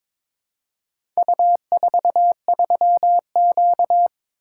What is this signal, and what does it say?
Morse code, heard as a keyed tone, about 22 wpm, 700 Hz. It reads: U43Q
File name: practice / 22wpm / Four Letter-Number Combinations - 22wpm